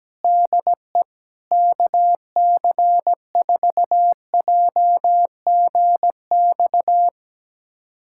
Morse code, heard as a keyed tone, about 17 wpm, 700 Hz. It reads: DE KC4JGX
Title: USA Callsign DE Callsign - 6 Characters - 17wpm